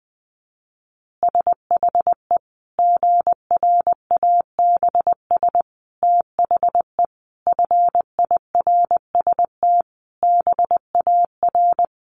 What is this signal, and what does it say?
Morse code, heard as a keyed tone, about 20 wpm, 700 Hz. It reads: SHE GRABS THE FIRST BAR
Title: Easy Phrases 20wpm